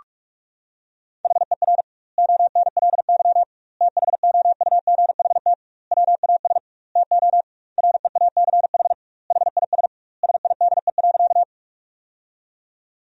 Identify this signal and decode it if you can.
Morse code, heard as a keyed tone, about 46 wpm, 700 Hz. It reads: HER ONLY THOUGHT WAS TO REACH HIS SIDE.
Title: 2x Word-Spacing - Sentences from Top 400 Words - 45wpm